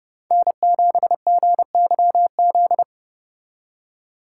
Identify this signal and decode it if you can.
Morse code, heard as a keyed tone, about 29 wpm, 700 Hz. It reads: N7GYZ